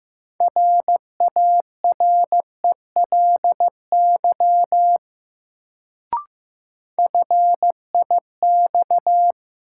Morse code, heard as a keyed tone, about 15 wpm, 700 Hz. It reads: RARELY FIX
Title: Sentences from Top 1400 Words - New Word Drill Alt - 15wpm